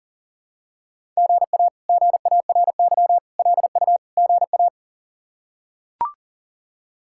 Morse code, heard as a keyed tone, about 40 wpm, 700 Hz. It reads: GA GARY LU GA
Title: K1USN SST QSO Practice - 40wpm